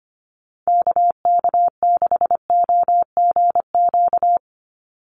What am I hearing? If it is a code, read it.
Morse code, heard as a keyed tone, about 25 wpm, 700 Hz. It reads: KK6OGQ
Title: USA Call Signs 25wpm